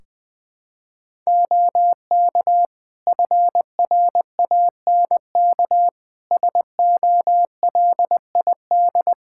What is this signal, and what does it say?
Morse code, heard as a keyed tone, about 20 wpm, 700 Hz. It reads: OK FRANK SOLID
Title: Practice QSO - Exchange 3 - More Info - 20wpm